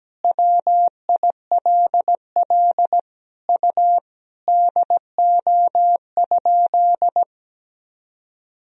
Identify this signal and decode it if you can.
Morse code, heard as a keyed tone, about 17 wpm, 700 Hz. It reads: WILL U DO?
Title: Sentences from Top 100 Words and Abbreviations 17wpm